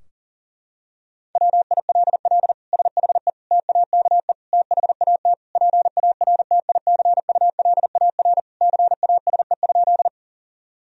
Morse code, heard as a keyed tone, about 40 wpm, 700 Hz. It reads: WILL SHE TAKE THAT PARTICULAR CASE?
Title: Sentences from Top 1000 Words - Review Part 4 - 40wpm